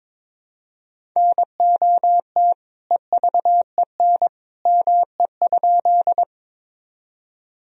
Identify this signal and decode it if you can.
Morse code, heard as a keyed tone, about 22 wpm, 700 Hz. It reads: NOT EVEN ME?
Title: Sentences from Top 1000 Words - Review Part 1 - 22wpm